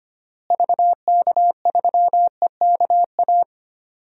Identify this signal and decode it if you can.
Morse code, heard as a keyed tone, about 25 wpm, 700 Hz. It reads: VK3EKA
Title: DXCC-100 Call Signs - 6 Characters - Repeated 2x - 25wpm